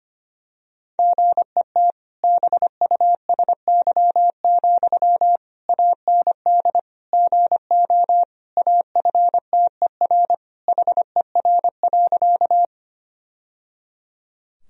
Morse code, heard as a keyed tone, about 25 wpm, 700 Hz. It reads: GET BUSY, AND GO AFTER HER.
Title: Sentences from Top 1000 Words - Review Part 3 - 25wpm